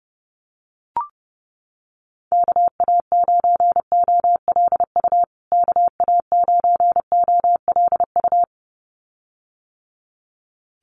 Morse code, heard as a keyed tone, about 30 wpm, 700 Hz. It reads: KA9OLU KA9OLU
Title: USA Call Signs Repeated 2x 30wpm